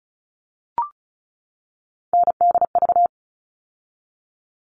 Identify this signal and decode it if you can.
Morse code, heard as a keyed tone, about 34 wpm, 700 Hz. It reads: NDV
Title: All Cut Numbers - 3-Digit 35wpm